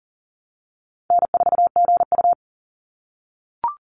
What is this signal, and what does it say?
Morse code, heard as a keyed tone, about 41 wpm, 700 Hz. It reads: N4GU
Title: Call Signs - 40wpm